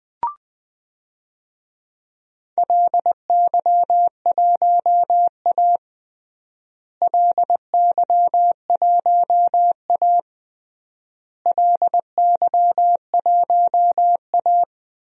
Morse code, heard as a keyed tone, about 20 wpm, 700 Hz. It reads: LY1A LY1A LY1A
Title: DXCC-100 Call Signs - 4 Characters - Repeated 3x - 20wpm